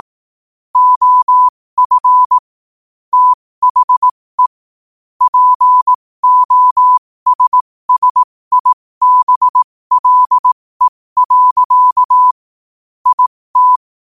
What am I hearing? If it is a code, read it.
Morse code, heard as a keyed tone, about 18 wpm, spaced as for 15 wpm, 1000 Hz. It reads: OF THE POSSIBLE. IT
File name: qotdfiles / 15wpm